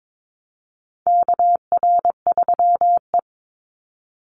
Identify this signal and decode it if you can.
Morse code, heard as a keyed tone, about 22 wpm, 700 Hz. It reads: KR3E
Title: Call Signs Repeated 2x - 22wpm